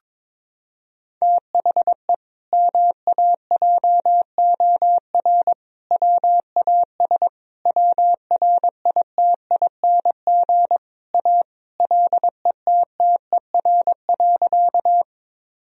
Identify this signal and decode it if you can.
Morse code, heard as a keyed tone, about 21 wpm, 700 Hz. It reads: THE MAJOR WAS WRITING A LETTER.
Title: Sentences from Top 1000 Words - Review Part 4 - 22wpm